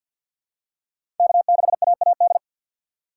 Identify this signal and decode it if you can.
Morse code, heard as a keyed tone, about 50 wpm, 700 Hz. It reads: K6AAD
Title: USA Call Signs - 5 Characters - 3x Character Spacing - 50wpm